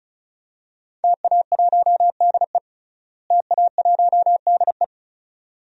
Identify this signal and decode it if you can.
Morse code, heard as a keyed tone, about 36 wpm, 700 Hz. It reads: TA1DE TA1DE
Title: DXCC-100 Call Signs - 5 Characters - Repeated 2x - 35wpm